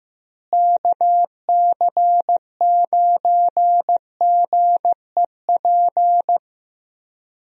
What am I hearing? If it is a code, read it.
Morse code, heard as a keyed tone, about 15 wpm, 700 Hz. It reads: KC9GEP
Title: USA Call Signs - 6 Characters - Repeated 2x - 15wpm